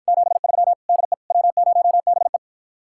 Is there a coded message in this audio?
Yes: Morse code from 0.1 to 2.4 s, heard as a keyed tone, about 52 wpm, 700 Hz. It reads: 73 DE W0BE